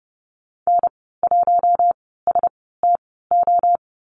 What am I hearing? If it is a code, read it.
Morse code, heard as a keyed tone, about 30 wpm, spaced as for 17 wpm, 700 Hz. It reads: N1STO
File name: Lesson 14g - Number 2 - Call Signs 30wpm